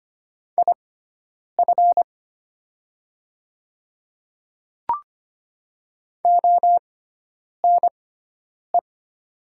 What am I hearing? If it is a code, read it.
Morse code, heard as a keyed tone, about 25 wpm, spaced as for 9 wpm, 700 Hz. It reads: IF ONE
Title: 6x Character Spacing - Top 500 Words - 25wpm